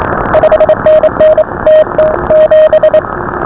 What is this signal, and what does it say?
Morse code, heard as a keyed tone, about 27 wpm, 590 Hz, with noise. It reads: 5NN TT7